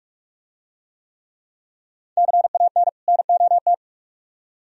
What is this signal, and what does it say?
Morse code, heard as a keyed tone, about 47 wpm, 700 Hz. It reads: CAN NOT